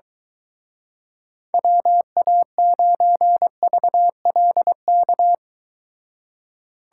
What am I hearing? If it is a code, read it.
Morse code, heard as a keyed tone, about 23 wpm, 700 Hz. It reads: WA9VLK